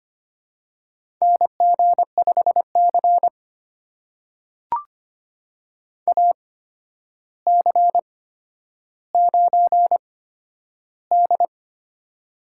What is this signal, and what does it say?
Morse code, heard as a keyed tone, about 25 wpm, spaced as for 11 wpm, 700 Hz. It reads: NG5C AC9D
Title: USA Call Signs - 4 Characters - 8x Character Spacing - 25wpm